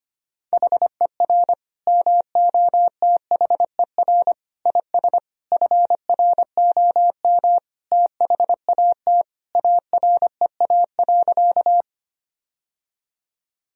Morse code, heard as a keyed tone, about 25 wpm, 700 Hz. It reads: HER MOTHER IS FROM THAT AREA.
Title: Sentences from Top 1000 Words - Review Part 2 - 25wpm